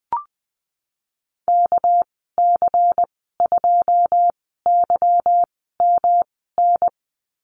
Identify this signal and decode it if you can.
Morse code, heard as a keyed tone, about 20 wpm, spaced as for 14 wpm, 700 Hz. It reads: KC2YMN
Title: 245 Lesson 39g - Letter z - Call Signs 20wpm